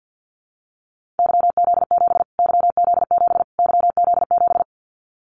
Morse code, heard as a keyed tone, about 49 wpm, 700 Hz. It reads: YZ7 YZ7 YZ7
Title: Callsign Prefixes for Top 25 DXCC - Repeated 3x - 50wpm